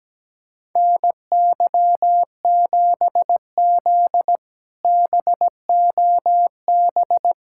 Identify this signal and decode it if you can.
Morse code, heard as a keyed tone, about 17 wpm, 700 Hz. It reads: NY7Z BOB
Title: K1USN SST QSO Practice - 17wpm